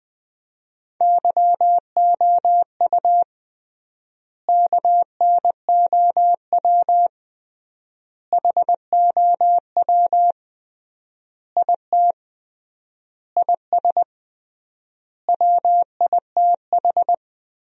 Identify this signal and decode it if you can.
Morse code, heard as a keyed tone, about 20 wpm, 700 Hz. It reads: YOU KNOW HOW IT IS WITH
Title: 3x Word-Spacing - Sentences from Top 100 Words - 20wpm